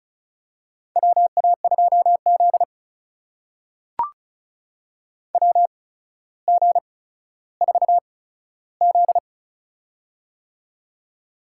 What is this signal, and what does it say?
Morse code, heard as a keyed tone, about 36 wpm, spaced as for 15 wpm, 700 Hz. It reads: WA2Z WG4Z
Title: USA Call Signs - 4 Characters - 8x Character Spacing - 35wpm